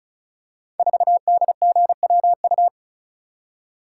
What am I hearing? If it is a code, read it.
Morse code, heard as a keyed tone, about 36 wpm, 700 Hz. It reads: 4DGWU
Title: Code Groups - 5 Letter:Numbers - 35wpm